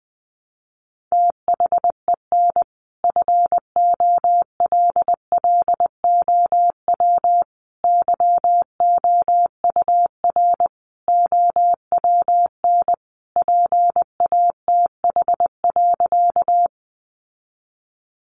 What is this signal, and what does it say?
Morse code, heard as a keyed tone, about 20 wpm, 700 Hz. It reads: THEN FOLLOW YOUR OWN PATH.
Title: Sentences from Top 1400 Words 20wpm